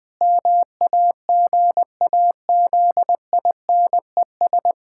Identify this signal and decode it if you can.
Morse code, heard as a keyed tone, about 20 wpm, 700 Hz. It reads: MAGAZINES
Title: Pangram Challenge v2 - 20wpm